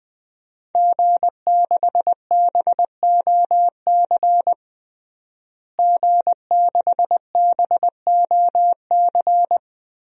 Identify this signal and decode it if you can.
Morse code, heard as a keyed tone, about 20 wpm, 700 Hz. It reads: G6BOC G6BOC
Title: DXCC-100 Call Signs - 5 Characters - Repeated 3x - 20wpm